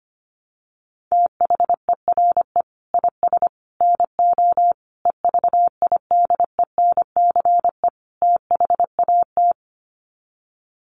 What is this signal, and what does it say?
Morse code, heard as a keyed tone, about 25 wpm, 700 Hz. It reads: THERE IS NO EVIDENCE THAT
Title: Sets of 5 Words Encore 25wpm